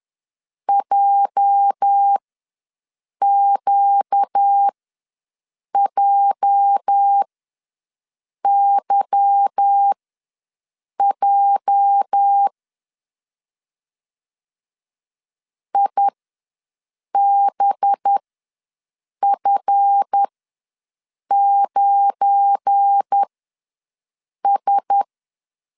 Morse code, heard as a keyed tone, about 10 wpm, spaced as for 6 wpm, 800 Hz. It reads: JQJYJ IBF9S
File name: LEZIONI / 8 YJ45 / 8 YJ ws 10wpm 4043